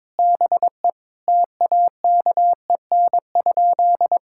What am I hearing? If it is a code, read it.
Morse code, heard as a keyed tone, about 22 wpm, 700 Hz. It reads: BE TAKEN?
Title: Sentences from Top 1000 Words - Review Part 6 - 22wpm